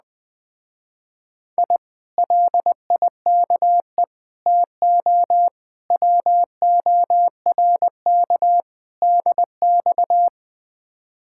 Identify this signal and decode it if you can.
Morse code, heard as a keyed tone, about 20 wpm, 700 Hz. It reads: I LIKE TO WORK DX
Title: Speed-Racing Common QSO Phrases - 17wpm